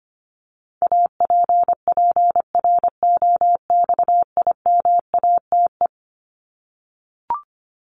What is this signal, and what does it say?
Morse code, heard as a keyed tone, about 25 wpm, 700 Hz. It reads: APPROXIMATE
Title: LY Suffix - Top 100 Words with Base Words 25wpm